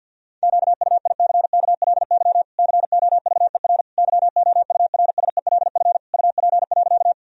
Morse code, heard as a keyed tone, about 50 wpm, 700 Hz. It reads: QUICKLY COVER YOURSELF UP.
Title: Sentences from Top 1000 Words - Review Part 1 - 50wpm